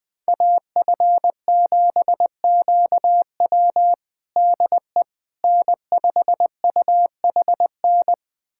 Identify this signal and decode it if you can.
Morse code, heard as a keyed tone, about 20 wpm, 700 Hz. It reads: AF7QW DE N5UHN
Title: USA Callsign DE Callsign - 5 Characters - 20wpm